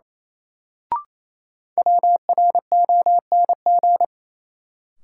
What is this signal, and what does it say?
Morse code, heard as a keyed tone, about 28 wpm, 700 Hz. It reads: WRONG
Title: Sets of 5 Words Encore - New Word Drill - 28wpm